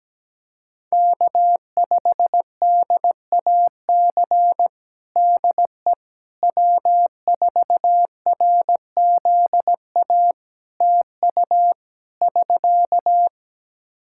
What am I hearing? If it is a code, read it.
Morse code, heard as a keyed tone, about 17 wpm, 700 Hz. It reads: K5DAC DE W4RZA TU <SK>
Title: Practice QSO - Exchange 4 - Ending - 17wpm